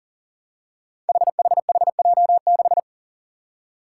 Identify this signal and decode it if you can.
Morse code, heard as a keyed tone, about 40 wpm, 700 Hz. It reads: HHHJ6
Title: Five Challenging Letter-Number Combinations 40wpm